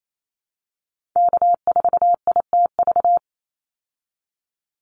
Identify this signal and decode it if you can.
Morse code, heard as a keyed tone, about 28 wpm, 700 Hz. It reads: K4ITV
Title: Call Signs Repeated 3x - 28wpm